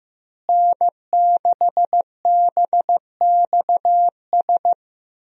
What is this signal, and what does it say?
Morse code, heard as a keyed tone, about 15 wpm, 700 Hz. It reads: N6BXS
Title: USA Call Signs - 5 Characters - Repeated 3x - 15wpm